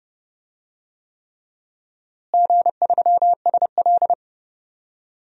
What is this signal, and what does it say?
Morse code, heard as a keyed tone, about 30 wpm, 700 Hz. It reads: G3SL
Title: DXCC-100 Call Signs - 4 Characters - Repeated 3x - 30wpm